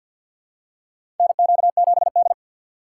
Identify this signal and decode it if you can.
Morse code, heard as a keyed tone, about 52 wpm, 700 Hz. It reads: NQ7D
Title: USA Call Signs - 4 Characters - 2x Character Spacing - 50wpm